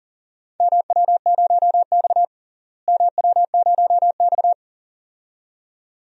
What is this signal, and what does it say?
Morse code, heard as a keyed tone, about 40 wpm, 700 Hz. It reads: MW0X MW0X
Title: DXCC-100 Call Signs - 4 Characters - Repeated 2x - 40wpm